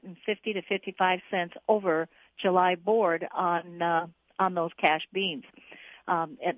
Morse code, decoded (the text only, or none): none